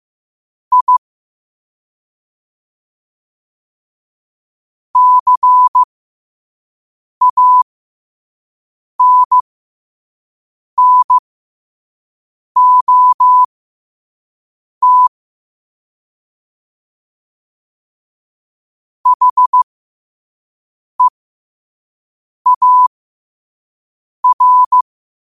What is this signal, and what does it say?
Morse code, heard as a keyed tone, about 15 wpm, spaced as for 5 wpm, 1000 Hz. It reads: I CANNOT HEAR
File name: qotdfiles / 5wpm